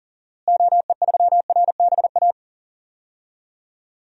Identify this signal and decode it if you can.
Morse code, heard as a keyed tone, about 40 wpm, 700 Hz. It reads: OE3RBA
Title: DXCC-100 Call Signs - 6 Characters - Repeated 3x - 40wpm